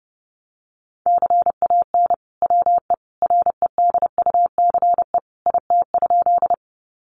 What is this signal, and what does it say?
Morse code, heard as a keyed tone, about 30 wpm, 700 Hz. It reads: CAN WE REDUCE IT?
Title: Sentences from Top 1100 Words 30wpm